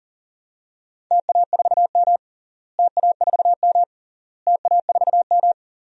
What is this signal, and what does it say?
Morse code, heard as a keyed tone, about 40 wpm, 700 Hz. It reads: TA4M TA4M TA4M
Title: DXCC-100 Call Signs - 4 Characters - Repeated 3x - 40wpm